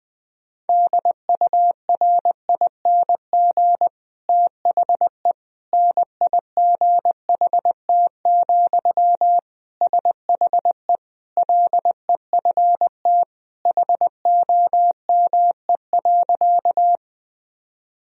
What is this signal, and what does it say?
Morse code, heard as a keyed tone, about 21 wpm, 700 Hz. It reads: DURING THE NIGHT, SHE LEFT HOME.
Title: Sentences from Top 1000 Words - Review Part 2 - 20wpm